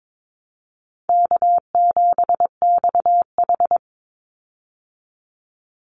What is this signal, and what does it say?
Morse code, heard as a keyed tone, about 22 wpm, 700 Hz. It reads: K7XH
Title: Call Signs Repeated 2x - 22wpm